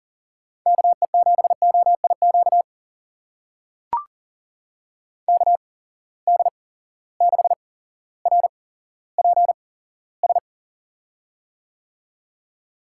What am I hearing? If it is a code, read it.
Morse code, heard as a keyed tone, about 40 wpm, spaced as for 17 wpm, 700 Hz. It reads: KE7OIQ KD6RPS
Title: USA Call Signs - 6 Characters - 8x Character Spacing - 40wpm